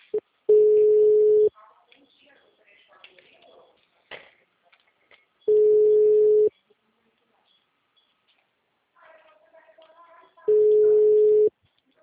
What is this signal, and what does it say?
Ringback tone (Europe, 425 Hz), with speech.
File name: out- -2005- - - .134